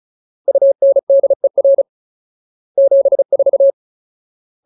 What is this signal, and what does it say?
Morse code, heard as a keyed tone, about 34 wpm, 550 Hz. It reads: UNDER 74
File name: Ultimate Word List / 35WPM Ultimate